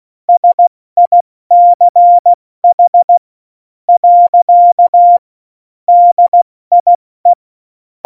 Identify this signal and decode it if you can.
Morse code, heard as a keyed tone, about 16 wpm, 700 Hz. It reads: SICH . DIE